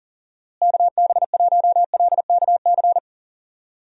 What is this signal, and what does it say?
Morse code, heard as a keyed tone, about 38 wpm, 700 Hz. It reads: KB1LKC
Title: USA Call Signs - 6 Characters - Repeated 2x - 40wpm